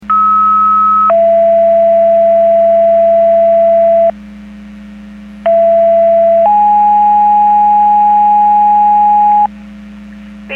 Two-tone page, heard as a one-second tone, then a three-second tone, about 1290 then 670 Hz, about 670 then 830 Hz. It is heard from 0.1 to 9.5 s.
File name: Portage 154.385 Paging